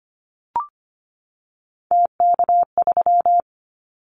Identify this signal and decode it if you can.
Morse code, heard as a keyed tone, about 25 wpm, 700 Hz. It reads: TK3
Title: Callsign Prefixes for Top 100 DXCC - 25wpm